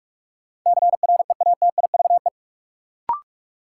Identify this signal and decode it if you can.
Morse code, heard as a keyed tone, about 46 wpm, 700 Hz. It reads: CREATIVE
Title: Sentences from Top 1000 Words - Review Part 5 - New Word Drill - 45wpm